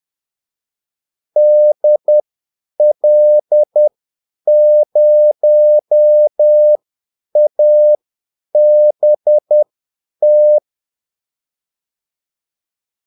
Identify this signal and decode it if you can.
Morse code, heard as a keyed tone, about 10 wpm, spaced as for 8 wpm, 600 Hz. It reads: DL0ABT
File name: Callsigns-10-8wpm